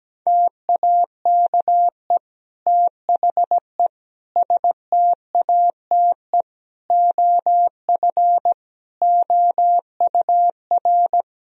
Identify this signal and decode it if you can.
Morse code, heard as a keyed tone, about 17 wpm, 700 Hz. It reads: TAKE THE STATE OF OUR
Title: Sentences from Top 1000 Words - Review Part 3 - 17wpm